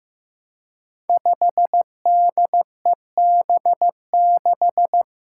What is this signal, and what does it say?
Morse code, heard as a keyed tone, about 15 wpm, 700 Hz. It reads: 5DEB6
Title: Five Challenging Letter-Number Combinations 15wpm